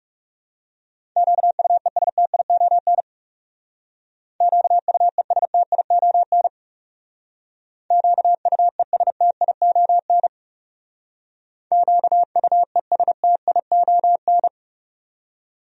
Morse code, heard as a keyed tone, about 40 wpm, 700 Hz. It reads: QUESTION QUESTION QUESTION QUESTION